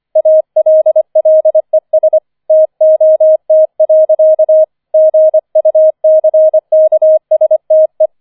Morse code, heard as a keyed tone, about 25 wpm, 630 Hz. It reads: ALLES TOT. GUCKSTE
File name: 005 Fünfter Streich 25wpm